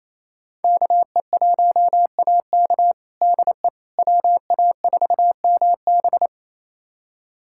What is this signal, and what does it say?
Morse code, heard as a keyed tone, about 27 wpm, 700 Hz. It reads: KE1AK DE WA4MB
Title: USA Callsign DE Callsign - 5 Characters - 28wpm